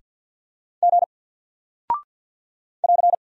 Morse code, heard as a keyed tone, about 50 wpm, 700 Hz. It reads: G <AR>
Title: Single Character (Letter:Number:Punctuation:Prosign) 50wpm